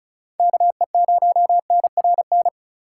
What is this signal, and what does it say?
Morse code, heard as a keyed tone, about 34 wpm, 690 Hz. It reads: KE0NRN
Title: USA Call Signs - 6 Characters - 3x Character Spacing - 35wpm